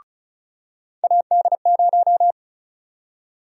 Morse code, heard as a keyed tone, about 36 wpm, 700 Hz. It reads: AD0
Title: USA Call Sign Prefixes - 35wpm